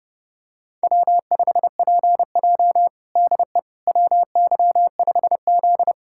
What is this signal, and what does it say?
Morse code, heard as a keyed tone, about 30 wpm, 700 Hz. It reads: W5PJ DE WY5Z
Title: USA Callsign DE Callsign - 4 Characters - 30wpm